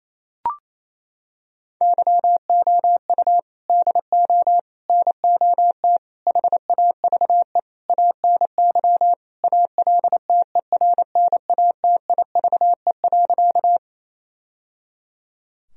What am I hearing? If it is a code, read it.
Morse code, heard as a keyed tone, about 28 wpm, 700 Hz. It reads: YOU DO NOT HAVE ANY ALTERNATIVE.
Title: Sentences from Top 1000 Words - Review Part 4 - 28wpm